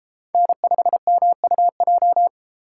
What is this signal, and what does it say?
Morse code, heard as a keyed tone, about 31 wpm, 700 Hz. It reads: N5MUJ